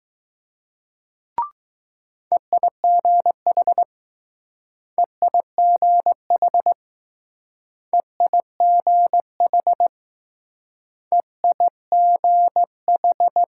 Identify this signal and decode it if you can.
Morse code, heard as a keyed tone, about 20 wpm, 700 Hz. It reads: EIGH EIGH EIGH EIGH